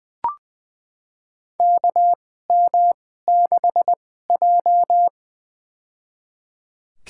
Morse code, heard as a keyed tone, about 20 wpm, spaced as for 15 wpm, 700 Hz. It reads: KM6J
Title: USA Call Signs - 4 Characters - 2x Character Spacing - 20wpm